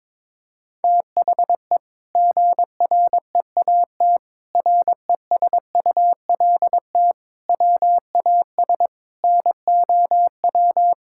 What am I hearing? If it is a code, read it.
Morse code, heard as a keyed tone, about 22 wpm, 700 Hz. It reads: THE GREAT RESULT WAS NOW
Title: Sentences from Top 1000 Words - Review Part 5 - 22wpm